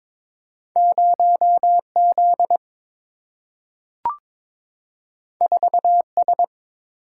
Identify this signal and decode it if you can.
Morse code, heard as a keyed tone, about 21 wpm, 700 Hz. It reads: 0 Z 4 S
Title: Single-Letter Single-Number 22wpm